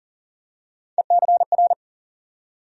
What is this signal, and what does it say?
Morse code, heard as a keyed tone, about 40 wpm, 700 Hz. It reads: ECR
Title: USA Call Sign Postfixes - 3 Letters 40wpm